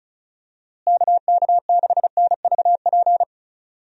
Morse code, heard as a keyed tone, about 34 wpm, 700 Hz. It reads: KK6NVP
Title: USA Call Signs - 6 Characters - Repeated 3x - 35wpm